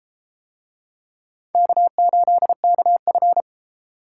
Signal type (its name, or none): Morse code